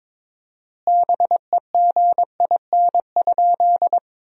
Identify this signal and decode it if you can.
Morse code, heard as a keyed tone, about 22 wpm, 700 Hz. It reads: BEGIN?
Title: 4x Word-Spacing - Sentences from Top 300 Words - 22wpm